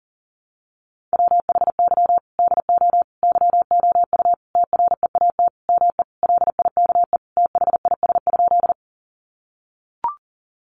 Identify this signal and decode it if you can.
Morse code, heard as a keyed tone, about 40 wpm, 700 Hz. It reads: WHY DO YOU TREAT ME LIKE THIS?
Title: Sentences from Top 1300 Words 40wpm